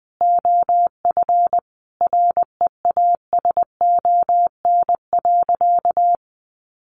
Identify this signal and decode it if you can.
Morse code, heard as a keyed tone, about 20 wpm, 700 Hz. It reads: OF REASON.
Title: Sentences from Top 1300 Words 20wpm